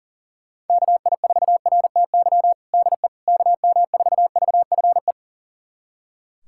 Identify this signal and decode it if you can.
Morse code, heard as a keyed tone, about 40 wpm, 700 Hz. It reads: KI4RTY DE KM4VFE
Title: USA Callsign DE Callsign - 6 Characters - 40wpm